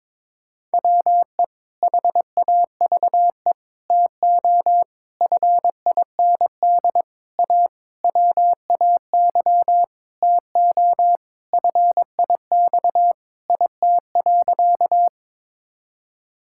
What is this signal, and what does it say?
Morse code, heard as a keyed tone, about 22 wpm, 700 Hz. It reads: WE HAVE TO FIND A WAY TO FIX IT.
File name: practice / 22wpm / Sentences with 7-Word Phrases - 22wpm